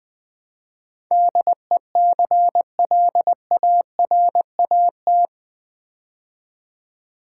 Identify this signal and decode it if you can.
Morse code, heard as a keyed tone, about 20 wpm, 700 Hz. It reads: DECLARAT
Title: Build A Word - 4 Letters At A Time - 11-Letter Words - 20wpm